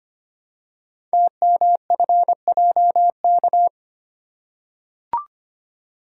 Morse code, heard as a keyed tone, about 25 wpm, 700 Hz. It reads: TMFJK
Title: Code Groups - 5 Letter:Numbers - 25wpm